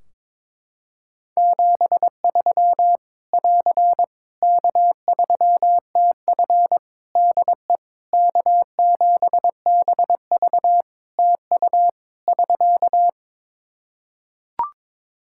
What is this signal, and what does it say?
Morse code, heard as a keyed tone, about 21 wpm, 700 Hz. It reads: 73 <AR> K3TF DE K7BV TU <SK>
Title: Practice QSO - Exchange 4 - Ending - 22wpm